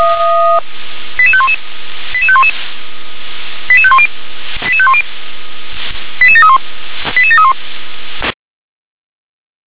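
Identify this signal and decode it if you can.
Five-tone selective call (ZVEI), heard as quick runs of five tones, tones of about 70 ms. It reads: 88511 88511 88510 88510 88731 88731
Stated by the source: ZVEI five-tone selective call received via SDR.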